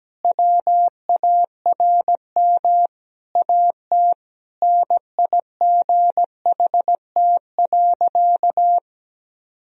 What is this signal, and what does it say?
Morse code, heard as a keyed tone, about 17 wpm, 700 Hz. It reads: WARM AT NIGHT.
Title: Sentences from Top 1000 Words - Review Part 2 - 17wpm